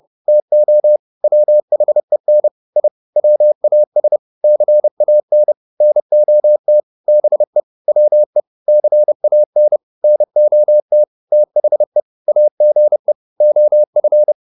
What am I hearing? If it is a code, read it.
Morse code, heard as a keyed tone, about 30 wpm, 600 Hz. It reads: TO WHEN I WAS CAN NOT BE WE CAN NOT THE AGE OF